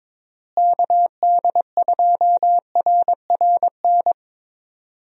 Morse code, heard as a keyed tone, about 22 wpm, 700 Hz. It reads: KD2RRN
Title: USA Call Signs - 6 Characters - Repeated 3x - 22wpm